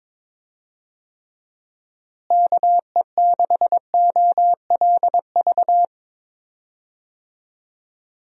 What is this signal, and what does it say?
Morse code, heard as a keyed tone, about 22 wpm, 700 Hz. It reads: KE6OLV
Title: USA Call Signs - 6 Characters - Repeated 3x - 22wpm